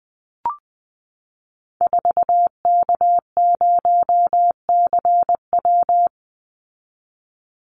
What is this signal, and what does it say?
Morse code, heard as a keyed tone, about 20 wpm, 700 Hz. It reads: 4K0CW
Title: Call Signs Repeated 3x - 20wpm